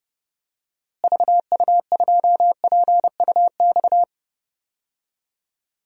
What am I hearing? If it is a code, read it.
Morse code, heard as a keyed tone, about 30 wpm, 700 Hz. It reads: VU2PUX
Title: DXCC-100 Call Signs - 6 Characters - Repeated 3x - 30wpm